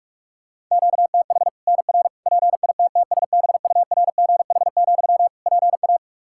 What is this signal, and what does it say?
Morse code, heard as a keyed tone, about 47 wpm, 700 Hz. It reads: QTH NR PITTSBURGH, PA
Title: Practice QSO - Exchange 1 - Beginning - 45wpm